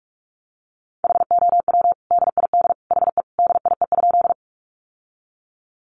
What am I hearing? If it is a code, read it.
Morse code, heard as a keyed tone, about 44 wpm, 700 Hz. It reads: HOW DID HE DIE?
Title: Sentences from Top 1300 Words 45wpm